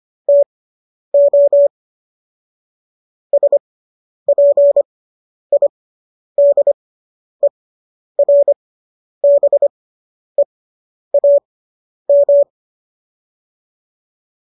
Morse code, heard as a keyed tone, about 25 wpm, spaced as for 10 wpm, 570 Hz. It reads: TO SPIDERBEAM